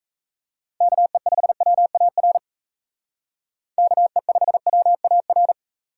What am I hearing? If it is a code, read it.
Morse code, heard as a keyed tone, about 36 wpm, 700 Hz. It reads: KE5WAR KE5WAR